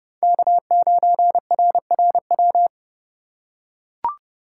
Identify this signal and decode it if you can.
Morse code, heard as a keyed tone, about 30 wpm, 700 Hz. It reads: K9RRW
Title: USA Call Signs - 5 Characters - 8x Character Spacing - 30wpm